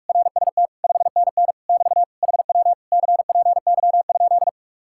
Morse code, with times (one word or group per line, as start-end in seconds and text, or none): 0.1-0.7 RST
0.8-1.5 5NN
1.7-2.0 <BT>
2.2-2.7 HW
2.9-4.5 CPY?